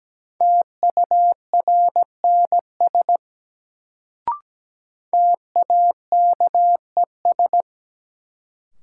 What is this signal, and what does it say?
Morse code, heard as a keyed tone, about 17 wpm, 700 Hz. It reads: TURNS TAKES
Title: Sets of 5 Words Encore - New Word Drill - 17wpm